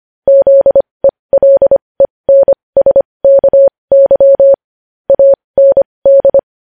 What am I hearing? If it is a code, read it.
Morse code, heard as a keyed tone, about 25 wpm, spaced as for 20 wpm, 550 Hz. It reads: ZELENSKY AND
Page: news headlines in morse code at 20 wpm